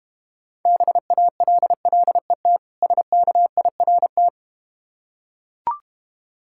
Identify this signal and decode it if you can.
Morse code, heard as a keyed tone, about 32 wpm, 700 Hz. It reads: BALLET SKIRT